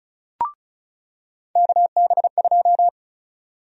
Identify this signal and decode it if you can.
Morse code, heard as a keyed tone, about 36 wpm, 700 Hz. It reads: KB2
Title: USA Call Sign Prefixes - 35wpm